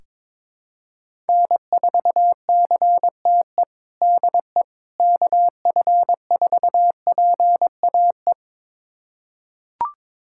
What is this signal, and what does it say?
Morse code, heard as a keyed tone, about 22 wpm, 700 Hz. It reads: N4CTE DE KF4PAE
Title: USA Callsign DE Callsign 22wpm